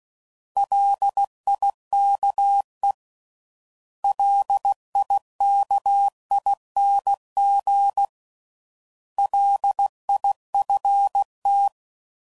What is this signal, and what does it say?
Morse code, heard as a keyed tone, about 16 wpm, 800 Hz. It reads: LIKE LIKING LIFT
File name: Prefixes / By First Letter / Ls